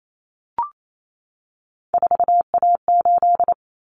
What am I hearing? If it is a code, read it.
Morse code, heard as a keyed tone, about 28 wpm, 700 Hz. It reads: 4A8
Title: Common Cut Numbers - 3-Digit 28wpm